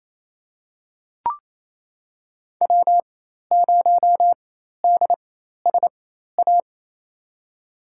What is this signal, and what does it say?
Morse code, heard as a keyed tone, about 28 wpm, spaced as for 13 wpm, 700 Hz. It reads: W0DSA
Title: USA Call Signs - 5 Characters - 4x Character Spacing - 28wpm